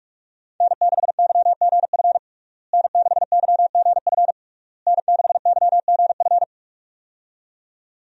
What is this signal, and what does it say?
Morse code, heard as a keyed tone, about 41 wpm, 700 Hz. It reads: N6YGF N6YGF N6YGF
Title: USA Call Signs - 5 Characters - Repeated 3x - 45wpm